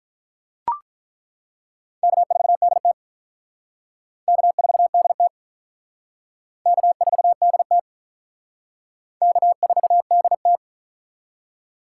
Morse code, heard as a keyed tone, about 48 wpm, 700 Hz. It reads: K4DT K4DT K4DT K4DT